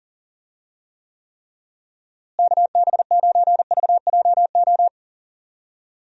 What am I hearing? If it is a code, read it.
Morse code, heard as a keyed tone, about 40 wpm, 700 Hz. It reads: KB9VJO
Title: USA Call Signs - 6 Characters - Repeated 3x - 40wpm